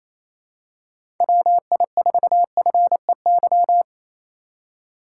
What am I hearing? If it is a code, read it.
Morse code, heard as a keyed tone, about 28 wpm, 700 Hz. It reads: WI4FEY
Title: USA Call Signs - 4x Character Spacing - 28wpm